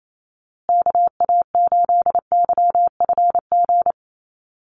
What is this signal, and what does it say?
Morse code, heard as a keyed tone, about 28 wpm, 700 Hz. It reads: KA8YFG
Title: USA Call Signs Repeated 2x 28wpm